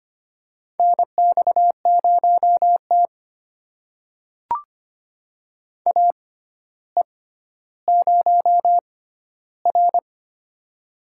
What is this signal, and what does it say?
Morse code, heard as a keyed tone, about 25 wpm, spaced as for 13 wpm, 700 Hz. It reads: NX0T AE0R
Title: USA Call Signs - 4 Characters - 6x Character Spacing - 25wpm